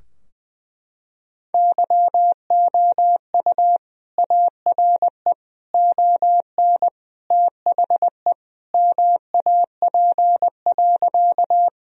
Morse code, heard as a keyed tone, about 20 wpm, 700 Hz. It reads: YOU ARE ON THE MAP.
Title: Sentences from Top 1000 Words - Review Part 2 - 20wpm